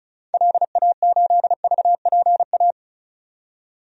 Morse code, heard as a keyed tone, about 34 wpm, 700 Hz. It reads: LA8VPA
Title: DXCC-100 Call Signs - 6 Characters - 35wpm